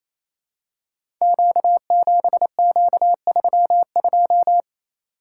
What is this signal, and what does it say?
Morse code, heard as a keyed tone, about 27 wpm, 700 Hz. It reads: Q7Q32
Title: Code Groups - 5 Letter:Numbers - 28wpm